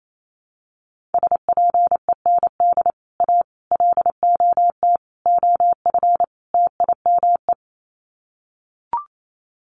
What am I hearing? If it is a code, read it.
Morse code, heard as a keyed tone, about 28 wpm, 700 Hz. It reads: SPEND A LOT OF TIME
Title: Sets of 5 Words 28wpm